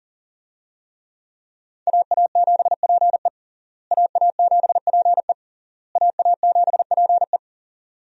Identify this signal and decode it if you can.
Morse code, heard as a keyed tone, about 37 wpm, 700 Hz. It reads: AA7PE AA7PE AA7PE
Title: USA Call Signs - 5 Characters - Repeated 3x - 40wpm